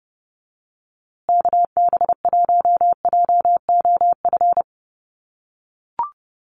Morse code, heard as a keyed tone, about 30 wpm, 700 Hz. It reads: KB1JOF
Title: USA Call Signs Repeated 2x 30wpm